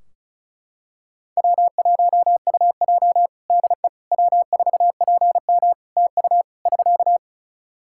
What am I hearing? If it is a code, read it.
Morse code, heard as a keyed tone, about 34 wpm, 700 Hz. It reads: W1UJ DE W4PM TU <SK>
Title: K1USN SST QSO Practice 35wpm